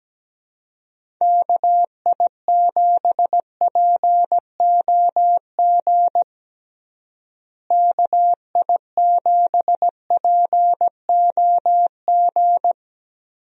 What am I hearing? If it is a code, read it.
Morse code, heard as a keyed tone, about 17 wpm, 700 Hz. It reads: KI7POG KI7POG
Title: USA Call Signs - 6 Characters - Repeated 3x - 17wpm